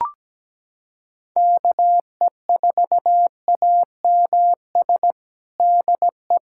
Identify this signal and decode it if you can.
Morse code, heard as a keyed tone, about 17 wpm, 700 Hz. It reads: KE4AMS DE
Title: USA Callsign DE Callsign 17wpm